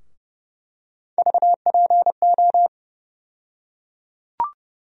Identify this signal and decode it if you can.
Morse code, heard as a keyed tone, about 32 wpm, 700 Hz. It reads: VPO
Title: USA Call Sign Postfixes - 3 Letters 30wpm